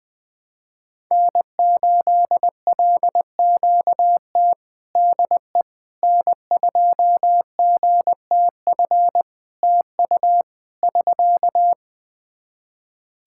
Morse code, heard as a keyed tone, about 20 wpm, 700 Hz. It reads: N8LQT DE N2GTF TU <SK>
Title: K1USN SST QSO Practice 20wpm